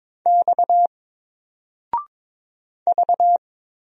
Morse code, heard as a keyed tone, about 22 wpm, 700 Hz. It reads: X V
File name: Single-Letter-Number-22wpm